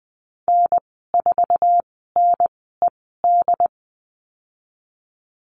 Morse code, heard as a keyed tone, about 20 wpm, spaced as for 15 wpm, 700 Hz. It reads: N4NED
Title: 088 Lesson 15g - Number 5 - Call Signs 20wpm